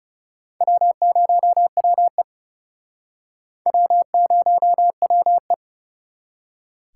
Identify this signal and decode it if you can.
Morse code, heard as a keyed tone, about 40 wpm, 700 Hz. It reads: W0WE W0WE